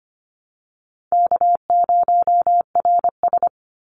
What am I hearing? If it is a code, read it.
Morse code, heard as a keyed tone, about 25 wpm, 700 Hz. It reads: K0RS
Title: NA SOTA Chaser Callsigns 25wpm